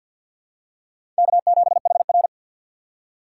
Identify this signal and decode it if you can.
Morse code, heard as a keyed tone, about 47 wpm, 700 Hz. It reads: K7HR
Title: USA Call Signs - 4x Character Spacing - 50wpm